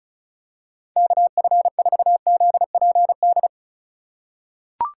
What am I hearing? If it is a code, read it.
Morse code, heard as a keyed tone, about 36 wpm, 700 Hz. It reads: KF4ZPD
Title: USA Call Signs - 6 Characters - 4x Character Spacing - 35wpm